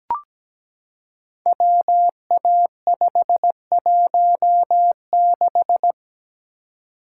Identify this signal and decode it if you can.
Morse code, heard as a keyed tone, about 17 wpm, 700 Hz. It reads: WA516
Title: Code Groups - 5 Letter:Numbers - 17wpm